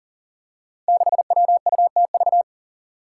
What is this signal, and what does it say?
Morse code, heard as a keyed tone, about 40 wpm, 700 Hz. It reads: 6WUTV
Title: Five Challenging Letter-Number Combinations 40wpm